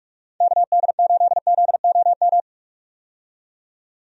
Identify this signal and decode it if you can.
Morse code, heard as a keyed tone, about 43 wpm, 700 Hz. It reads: KD8ZOM
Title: USA Call Signs - 6 Characters - 45wpm